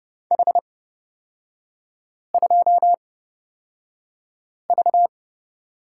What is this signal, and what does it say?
Morse code, heard as a keyed tone, about 30 wpm, spaced as for 5 wpm, 700 Hz. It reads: H2V
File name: Lesson 36e - BT - Three Characters 30wpm